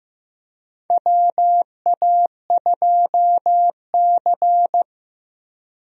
Morse code, heard as a keyed tone, about 15 wpm, 700 Hz. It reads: WA2C
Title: USA Call Signs - 4 Characters - 15wpm